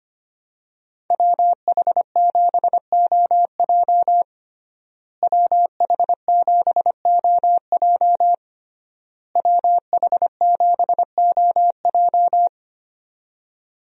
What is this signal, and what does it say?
Morse code, heard as a keyed tone, about 25 wpm, 700 Hz. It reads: WH7OJ WH7OJ WH7OJ
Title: USA Call Signs - 5 Characters - Repeated 3x - 25wpm